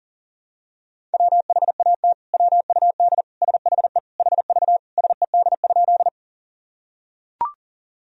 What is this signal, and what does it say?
Morse code, heard as a keyed tone, about 40 wpm, 700 Hz. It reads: WHAT WUD SHE HV SED?
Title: Sentences from Top 100 Words and Abbreviations 40wpm